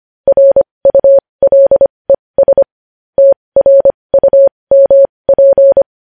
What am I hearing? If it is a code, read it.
Morse code, heard as a keyed tone, about 25 wpm, spaced as for 20 wpm, 550 Hz. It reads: RULES TRUMP
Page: news-headlines-in-morse-code-at-20-wpm